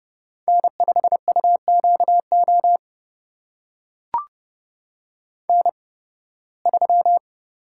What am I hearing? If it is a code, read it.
Morse code, heard as a keyed tone, about 32 wpm, spaced as for 20 wpm, 700 Hz. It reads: N5UQO N3
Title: USA Call Signs - 5 Characters - 8x Character Spacing - 30wpm